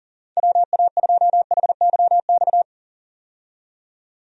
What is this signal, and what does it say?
Morse code, heard as a keyed tone, about 40 wpm, 700 Hz. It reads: WA2HYX
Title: USA Call Signs - 6 Characters - 40wpm